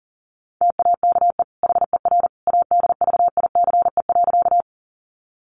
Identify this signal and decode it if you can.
Morse code, heard as a keyed tone, about 40 wpm, 700 Hz. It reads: TAKE HER ADVICE.
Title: Sentences from Top 1100 Words 40wpm